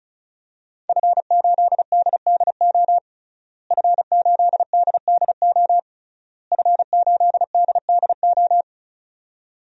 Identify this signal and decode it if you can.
Morse code, heard as a keyed tone, about 36 wpm, 700 Hz. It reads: F8DDO F8DDO F8DDO
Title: DXCC-100 Call Signs - 5 Characters - Repeated 3x - 35wpm